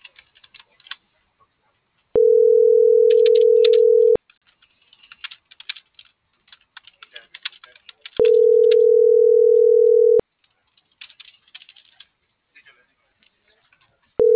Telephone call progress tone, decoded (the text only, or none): ringback tone